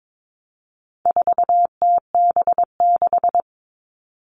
Morse code, heard as a keyed tone, about 22 wpm, 700 Hz. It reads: 4TB6
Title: All Cut Numbers - 4-Digit 22wpm